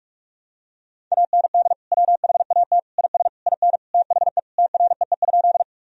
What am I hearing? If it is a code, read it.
Morse code, heard as a keyed tone, about 43 wpm, 700 Hz. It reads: AND WHAT IS IN THE TREE?
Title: Sentences from Top 1000 Words - Review Part 1 - 45wpm